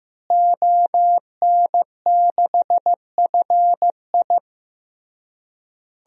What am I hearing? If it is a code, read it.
Morse code, heard as a keyed tone, about 15 wpm, 700 Hz. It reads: ON6FI
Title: DXCC-100 Call Signs - 5 Characters - Repeated 3x - 15wpm